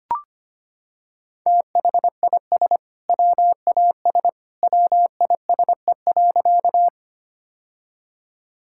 Morse code, heard as a keyed tone, about 25 wpm, 700 Hz. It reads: THIS WAS WISE.
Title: Sentences from Top 1600 Words 25wpm